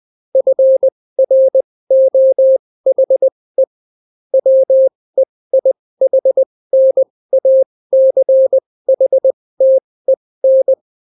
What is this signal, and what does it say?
Morse code, heard as a keyed tone, about 20 wpm, spaced as for 16 wpm, 530 Hz. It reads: FROHE WEIHNACHTEN